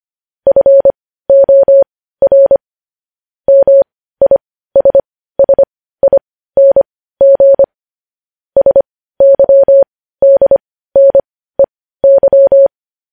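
Morse code, heard as a keyed tone, about 25 wpm, spaced as for 15 wpm, 550 Hz. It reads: FOR MISSING SYDNEY
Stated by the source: News headlines sent in Morse code at 15 wpm.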